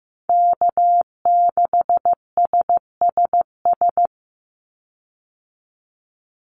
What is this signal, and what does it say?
Morse code, heard as a keyed tone, about 15 wpm, 700 Hz. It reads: K6SSS
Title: Call Signs Repeated 3x - 15wpm